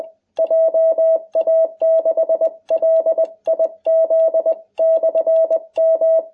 Morse code, heard as a keyed tone, about 20 wpm, 660 Hz. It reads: JA6LIZ/M